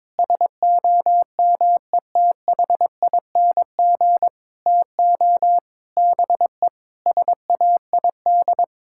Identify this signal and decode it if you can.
Morse code, heard as a keyed tone, about 22 wpm, 700 Hz. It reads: SOMETHING TO BE SAID
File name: practice / 22wpm / Sets of 7 Words - 22wpm